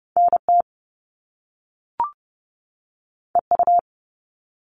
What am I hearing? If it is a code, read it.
Morse code, heard as a keyed tone, about 30 wpm, 700 Hz. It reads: NT EU
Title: Top Two Letter Combinations in Words 30wpm